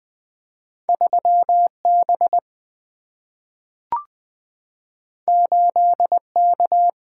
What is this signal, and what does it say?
Morse code, heard as a keyed tone, about 20 wpm, 700 Hz. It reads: 3B 8K
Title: Single-Letter Single-Number 20wpm